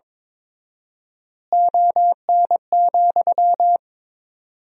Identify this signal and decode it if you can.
Morse code, heard as a keyed tone, about 22 wpm, 700 Hz. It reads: ON,
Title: 4x Word-Spacing - Sentences from Top 200 Words - 22wpm